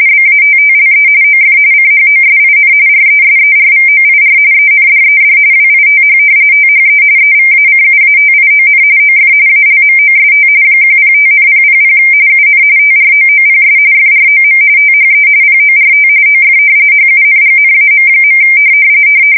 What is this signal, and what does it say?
RTTY, about 45.45 baud. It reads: BATTLE OF CHOSIN RESERVOIR ENDED WITH THE EVACUATION OF ALMOST 105,000 UN TROOPS FROM THE PORT OF HUNGNAM.
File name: fsk170 / usnavy5